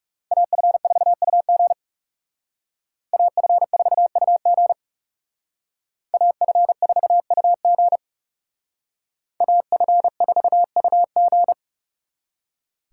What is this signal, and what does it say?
Morse code, heard as a keyed tone, about 38 wpm, 700 Hz. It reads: AF4UG AF4UG AF4UG AF4UG